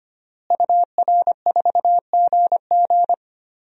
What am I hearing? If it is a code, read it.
Morse code, heard as a keyed tone, about 25 wpm, 700 Hz. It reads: UR4GG
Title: DXCC-100 Call Signs - 5 Characters - Repeated 3x - 25wpm